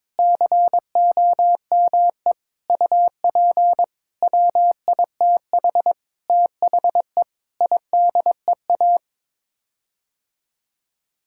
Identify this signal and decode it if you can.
Morse code, heard as a keyed tone, about 22 wpm, 700 Hz. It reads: COME UP WITH THE IDEA
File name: practice / 22wpm / Sets of 8 Words - 22wpm